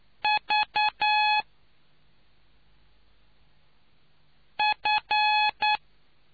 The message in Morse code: V F

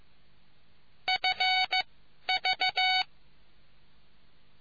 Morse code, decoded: FV

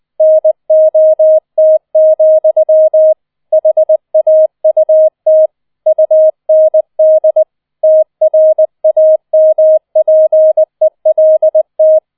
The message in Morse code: NOT, HAUT UND TRAMPELT

